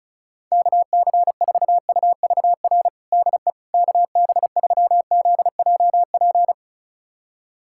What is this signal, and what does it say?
Morse code, heard as a keyed tone, about 36 wpm, 700 Hz. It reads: KC4UVR DE KB3ZJP
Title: USA Callsign DE Callsign 35wpm